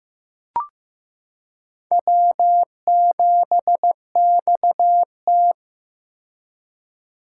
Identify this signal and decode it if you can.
Morse code, heard as a keyed tone, about 15 wpm, 700 Hz. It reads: W7XT
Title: K1USN SST QSO Practice - 15wpm